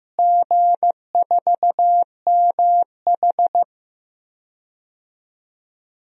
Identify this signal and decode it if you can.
Morse code, heard as a keyed tone, about 15 wpm, 700 Hz. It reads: G4MH
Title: DXCC-100 Call Signs - 4 Characters - Repeated 2x - 15wpm